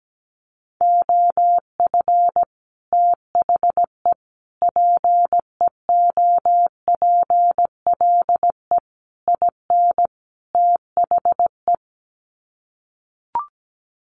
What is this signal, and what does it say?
Morse code, heard as a keyed tone, about 17 wpm, 700 Hz. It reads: OF THE PEOPLE IN THE
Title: Sets of 5 Words Encore 17wpm